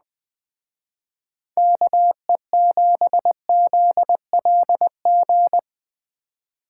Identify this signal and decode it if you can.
Morse code, heard as a keyed tone, about 20 wpm, 700 Hz. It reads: KE7ZLG